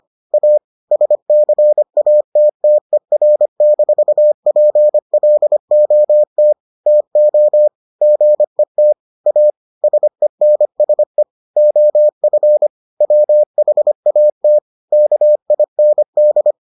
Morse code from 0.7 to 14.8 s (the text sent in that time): SCATTER-PLOT TO GET A SENSE OF WHAT